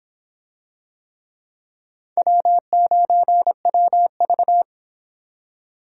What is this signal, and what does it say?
Morse code, heard as a keyed tone, about 25 wpm, 700 Hz. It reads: W9WV